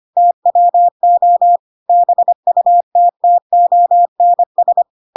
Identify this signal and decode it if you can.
Morse code, heard as a keyed tone, about 25 wpm, 700 Hz. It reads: TWO BUTTONS
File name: Pinocchio / Pinocchio ch2 25 25 ik1hge